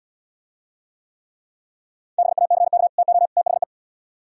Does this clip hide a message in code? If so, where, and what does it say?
Yes: Morse code from 2.2 to 3.6 s, heard as a keyed tone, about 76 wpm, 700 Hz. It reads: BACK TO THE